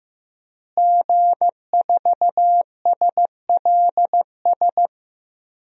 Morse code, heard as a keyed tone, about 15 wpm, 700 Hz. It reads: G4SLS
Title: DXCC-100 Call Signs - 5 Characters - Repeated 3x - 15wpm